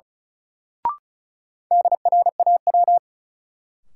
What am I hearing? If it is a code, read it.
Morse code, heard as a keyed tone, about 34 wpm, 700 Hz. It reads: DRAW
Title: Sentences from Top 300 Words - New Word Drill - 35wpm